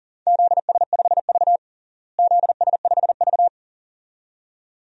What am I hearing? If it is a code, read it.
Morse code, heard as a keyed tone, about 40 wpm, 700 Hz. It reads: ZS5V ZS5V
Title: DXCC-100 Call Signs - 4 Characters - Repeated 2x - 40wpm